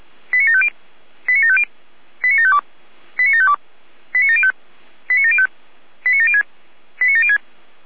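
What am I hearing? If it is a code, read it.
Five-tone selective call (ZVEI), heard as quick runs of five tones, tones of about 70 ms. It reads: 89750 89750 89752 89752 89705 89705 89706 89706
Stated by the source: ZVEI five-tone selective call 89715 received by SDR.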